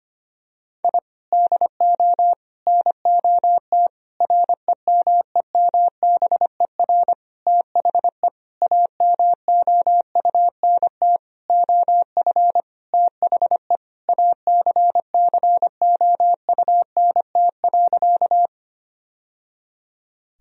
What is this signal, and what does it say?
Morse code, heard as a keyed tone, about 25 wpm, 700 Hz. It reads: I DO NOT REMEMBER THE AMOUNT OF THE ACCOUNT.
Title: Sentences from Top 1000 Words - Review Part 4 - 25wpm